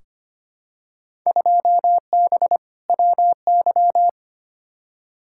Morse code, heard as a keyed tone, about 25 wpm, 700 Hz. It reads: 2B WY
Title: Field Day Exchanges Encore - 25wpm